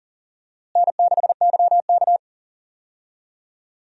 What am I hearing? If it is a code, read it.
Morse code, heard as a keyed tone, about 38 wpm, 700 Hz. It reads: N6YK
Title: USA Call Signs - 4 Characters - Repeated 3x - 40wpm